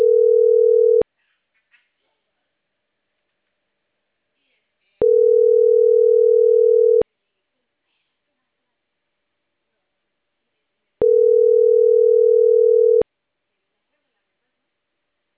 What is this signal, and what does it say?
Ringback tone (North America), 440 + 480 Hz.